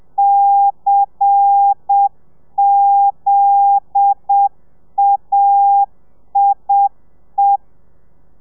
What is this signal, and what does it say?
Morse code, heard as a keyed tone, about 7 wpm, 800 Hz. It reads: CZAIE
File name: telegrafia / i2tfj / Lez10 30